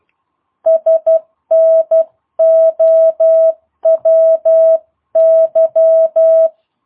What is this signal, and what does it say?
Morse code, heard as a keyed tone, about 12 wpm, 660 Hz. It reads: SNOWY
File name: jh8cbh / 2022cw